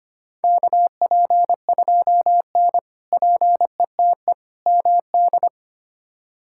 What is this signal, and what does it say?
Morse code, heard as a keyed tone, about 25 wpm, 700 Hz. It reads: KP2N PETE MD
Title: K1USN SST QSO Practice - 25wpm